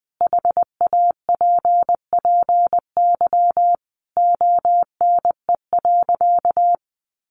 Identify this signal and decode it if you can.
Morse code, heard as a keyed tone, about 20 wpm, 700 Hz. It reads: HAPPY ONE.
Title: Sentences from Top 1400 Words 20wpm